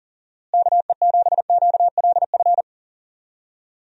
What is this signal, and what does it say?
Morse code, heard as a keyed tone, about 37 wpm, 700 Hz. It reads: KE7QLF
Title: USA Call Signs - 6 Characters - Repeated 2x - 40wpm